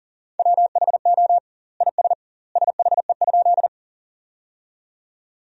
Morse code, heard as a keyed tone, about 40 wpm, 700 Hz. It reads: WHO IS SHE?
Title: 2x Word-Spacing - Sentences from Top 100 Words - 40wpm